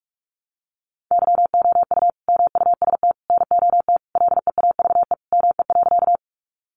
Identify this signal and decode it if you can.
Morse code, heard as a keyed tone, about 44 wpm, 700 Hz. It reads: YOU MUST NOT LEAVE ME.
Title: Sentences from Top 600 Words 45wpm